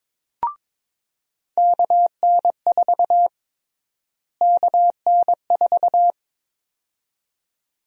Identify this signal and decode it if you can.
Morse code, heard as a keyed tone, about 22 wpm, 700 Hz. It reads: KN4 KN4
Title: USA Call Sign Prefixes - Repeated 2x - 22wpm